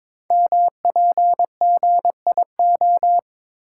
Morse code, heard as a keyed tone, about 22 wpm, 700 Hz. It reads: MPGIO
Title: Code Groups - 5 Letters - 22wpm